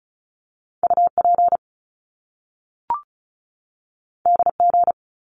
Morse code, heard as a keyed tone, about 36 wpm, 700 Hz. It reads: UP DG
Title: Top Two Letter Combinations in Words 35wpm